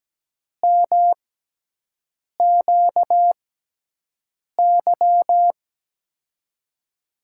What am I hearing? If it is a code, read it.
Morse code, heard as a keyed tone, about 17 wpm, spaced as for 6 wpm, 700 Hz. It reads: MQY
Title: USA Call Signs - 6x Character Spacing - 17wpm